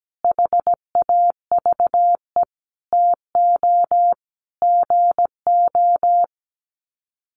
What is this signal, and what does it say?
Morse code, heard as a keyed tone, about 17 wpm, 700 Hz. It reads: HAVE TO GO
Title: Sets of 5 Words Encore 17wpm